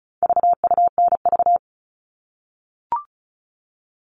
Morse code, heard as a keyed tone, about 34 wpm, 700 Hz. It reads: VUNV
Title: All Cut Numbers - 4-Digit 35wpm